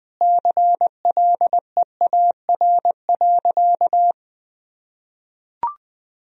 Morse code, heard as a keyed tone, about 21 wpm, 700 Hz. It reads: CLEAR.
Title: Sentences from Top 500 Words and Abbreviations 20wpm